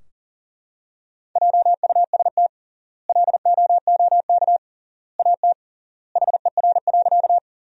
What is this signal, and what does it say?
Morse code, heard as a keyed tone, about 40 wpm, 700 Hz. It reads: JUST LOOK AT HER.
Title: 3x Word-Spacing - Sentences from Top 200 Words - 40wpm